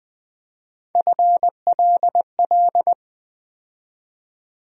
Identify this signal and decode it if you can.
Morse code, heard as a keyed tone, about 20 wpm, 700 Hz. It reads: FLL
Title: USA Call Sign Postfixes - 3 Letters 20wpm